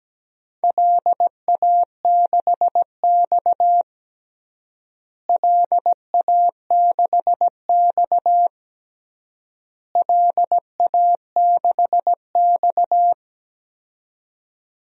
Morse code, heard as a keyed tone, about 17 wpm, 700 Hz. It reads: LA6X LA6X LA6X
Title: DXCC-100 Call Signs - 4 Characters - Repeated 3x - 17wpm